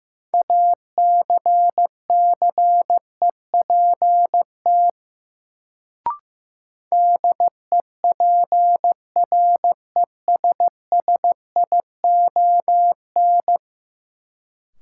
Morse code, heard as a keyed tone, about 15 wpm, 700 Hz. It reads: ACCEPT DEPRESSION
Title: Sentences from Top 1300 Words - New Word Drill - 15wpm